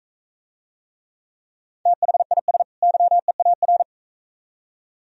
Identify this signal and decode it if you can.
Morse code, heard as a keyed tone, about 41 wpm, 700 Hz. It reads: THIS YEAR